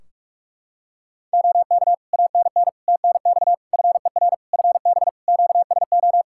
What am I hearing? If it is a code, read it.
Morse code, heard as a keyed tone, about 46 wpm, 700 Hz. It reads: OK ANN TNX FER FB QSO
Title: K1USN SST QSO Practice 45wpm